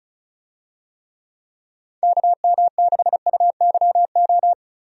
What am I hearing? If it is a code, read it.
Morse code, heard as a keyed tone, about 34 wpm, 700 Hz. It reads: KM6UYO
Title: USA Call Signs - 6 Characters - 35wpm